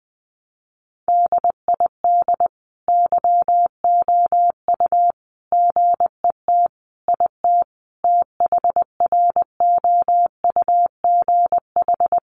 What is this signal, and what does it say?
Morse code, heard as a keyed tone, about 20 wpm, 700 Hz. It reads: DID YOU GET IT THROUGH
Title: Sentences from Top 1300 Words 20wpm